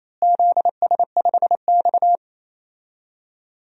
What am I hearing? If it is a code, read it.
Morse code, heard as a keyed tone, about 28 wpm, 700 Hz. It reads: ZS5X
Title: DXCC-100 Call Signs - 4 Characters - Repeated 3x - 28wpm